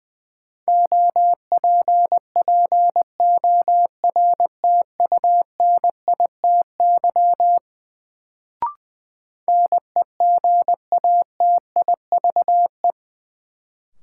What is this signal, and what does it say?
Morse code, heard as a keyed tone, about 20 wpm, 700 Hz. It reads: OPPORTUNITY NEGATIVE
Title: Sentences from Top 1000 Words - New Word Drill - 20wpm